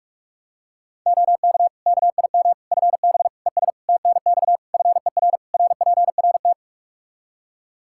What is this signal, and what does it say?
Morse code, heard as a keyed tone, about 46 wpm, 700 Hz. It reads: OK KIM FB ES TNX FER RPRT
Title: Practice QSO - Exchange 2 - Info - 45wpm